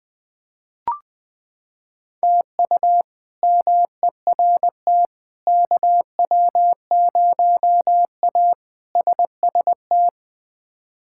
Morse code, heard as a keyed tone, about 20 wpm, 700 Hz. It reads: TU MERT KW0A SST
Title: K1USN SST QSO Practice - 20wpm